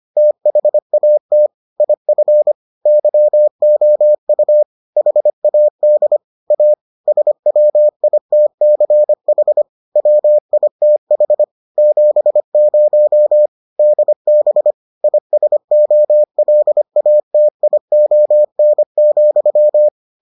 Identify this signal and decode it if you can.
Morse code, heard as a keyed tone, about 25 wpm, 600 Hz. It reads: THAT IF YOU HAD A SWITCH WITH 70 DB ISOLATION,